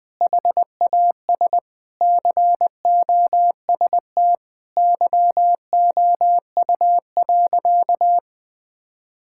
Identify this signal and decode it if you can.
Morse code, heard as a keyed tone, about 20 wpm, 700 Hz. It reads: HAS COST YOU.
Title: Sentences from Top 1000 Words - Review Part 5 - 20wpm